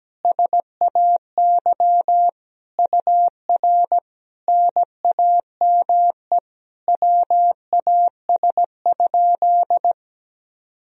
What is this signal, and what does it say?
Morse code, heard as a keyed tone, about 17 wpm, 700 Hz. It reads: SAY UR NAME WAS?
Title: Sentences from Top 200 Words and Abbreviations 17wpm